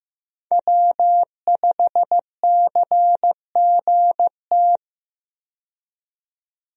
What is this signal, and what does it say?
Morse code, heard as a keyed tone, about 15 wpm, 700 Hz. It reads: W5CGT
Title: USA Call Signs - 5 Characters - 15wpm